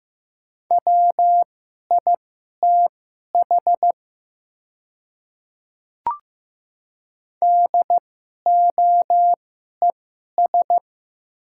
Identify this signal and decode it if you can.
Morse code, heard as a keyed tone, about 15 wpm, spaced as for 11 wpm, 700 Hz. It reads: WITH DOES
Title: 2x Character Spacing - Top 200 Words - 15wpm